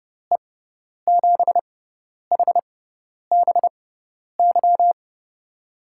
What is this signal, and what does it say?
Morse code, heard as a keyed tone, about 30 wpm, spaced as for 10 wpm, 700 Hz. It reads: E7HBY
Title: USA Call Signs - 6 Characters - 6x Character Spacing - 30wpm